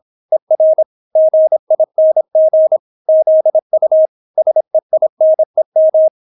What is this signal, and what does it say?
Morse code, heard as a keyed tone, about 26 wpm, 630 Hz. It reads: ER GING ZU SEINEM